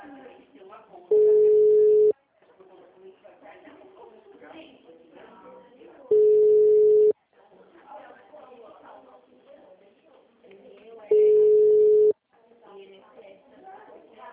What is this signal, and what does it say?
Ringback tone (Europe, 425 Hz), with speech.
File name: out- -2001- - - .9398